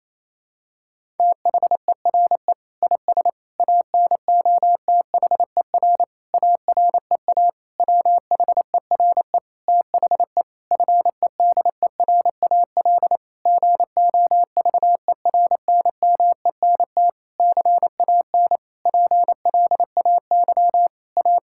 Morse code, heard as a keyed tone, about 28 wpm, 700 Hz. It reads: THERE IS ANOTHER AREA WHERE THE FEDERAL GOVERNMENT CAN PLAY A
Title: Sentences from Top 1000 Words - Review Part 5 - 28wpm